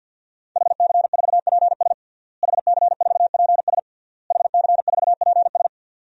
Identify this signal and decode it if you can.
Morse code, heard as a keyed tone, about 50 wpm, 700 Hz. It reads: HC4PS HC4PS HC4PS
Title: DXCC-100 Call Signs - 5 Characters - Repeated 3x - 50wpm